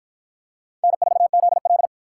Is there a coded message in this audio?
Yes: Morse code from 0.8 to 1.9 s, heard as a keyed tone, about 53 wpm, 700 Hz. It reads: N4ZL